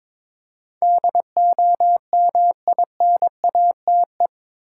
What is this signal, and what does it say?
Morse code, heard as a keyed tone, about 22 wpm, 700 Hz. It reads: DOMINATE
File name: practice / 22wpm / Eight Letter Words - 22wpm